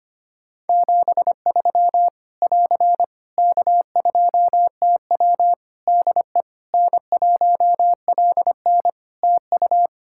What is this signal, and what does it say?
Morse code, heard as a keyed tone, about 25 wpm, 700 Hz. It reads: 73 <AR> K2TW DE N1LN TU
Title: Practice QSO - Exchange 4 - Ending - 25wpm